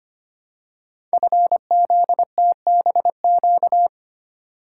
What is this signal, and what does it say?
Morse code, heard as a keyed tone, about 25 wpm, 700 Hz. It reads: FZTBQ
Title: Code Groups - 5 Letters - 25wpm